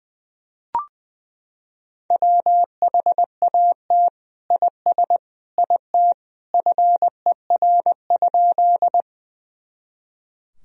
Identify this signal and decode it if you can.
Morse code, heard as a keyed tone, about 20 wpm, 700 Hz. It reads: WHAT IS IT FER?
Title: Sentences from Top 100 Words and Abbreviations 20wpm